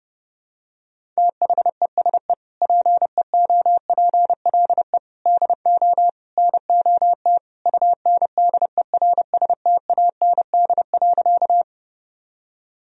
Morse code, heard as a keyed tone, about 30 wpm, 700 Hz. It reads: THESE PEOPLE DO NOT UNDERSTAND.
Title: Sentences from Top 1000 Words - Review Part 1 - 30wpm